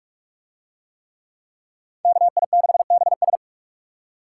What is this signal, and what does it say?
Morse code, heard as a keyed tone, about 41 wpm, 700 Hz. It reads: KI6BS